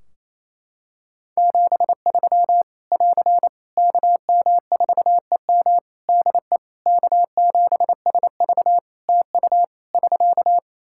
Morse code, heard as a keyed tone, about 28 wpm, 700 Hz. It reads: 73 <AR> KM4EM DE K7SV TU <SK>
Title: Practice QSO - Exchange 4 - Ending - 28wpm